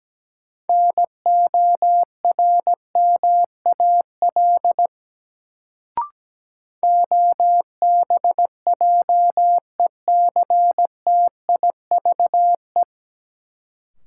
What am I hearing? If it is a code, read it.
Morse code, heard as a keyed tone, about 17 wpm, 700 Hz. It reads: NORMAL OBJECTIVE
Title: Sentences from Top 1200 Words - New Word Drill Alt - 17wpm